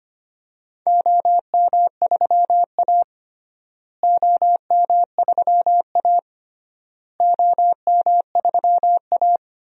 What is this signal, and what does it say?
Morse code, heard as a keyed tone, about 25 wpm, 700 Hz. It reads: OM3A OM3A OM3A
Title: DXCC-100 Call Signs - 4 Characters - Repeated 3x - 25wpm